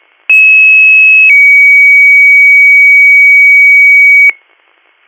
Two-tone page, heard as a one-second tone, then a three-second tone, about 2600 then 2250 Hz, with noise. It is heard from 0.3 to 4.3 s.